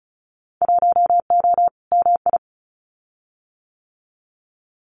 Morse code, heard as a keyed tone, about 36 wpm, 700 Hz. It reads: 1O MI
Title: Winter Field Day Exchanges 35wpm